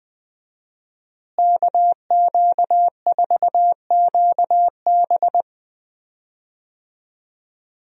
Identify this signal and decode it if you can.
Morse code, heard as a keyed tone, about 20 wpm, 700 Hz. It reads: KQ4QB
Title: USA Call Signs - 5 Characters - Repeated 2x - 20wpm